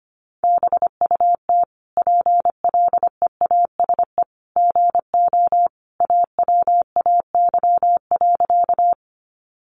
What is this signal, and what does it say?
Morse code, heard as a keyed tone, about 25 wpm, 700 Hz. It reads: BUT PLEASE GO AWAY.
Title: Sentences from Top 1500 Words 25wpm